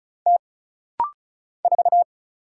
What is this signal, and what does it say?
Morse code, heard as a keyed tone, about 36 wpm, 700 Hz. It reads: T 4
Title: Single Letter-Number - Rapid-Fire 35wpm